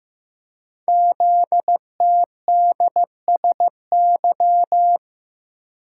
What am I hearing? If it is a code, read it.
Morse code, heard as a keyed tone, about 15 wpm, 700 Hz. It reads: ZTDSY
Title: Code Groups - 5 Letters - 15wpm